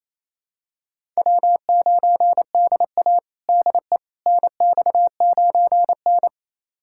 Morse code, heard as a keyed tone, about 28 wpm, 700 Hz. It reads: W9DA DE NX9N
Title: USA Callsign DE Callsign - 4 Characters - 28wpm